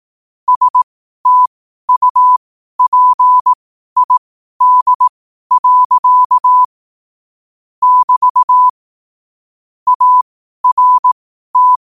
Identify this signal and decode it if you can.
Morse code, heard as a keyed tone, about 18 wpm, spaced as for 12 wpm, 1000 Hz. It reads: STUPID. <BT> ART
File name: qotdfiles / 12wpm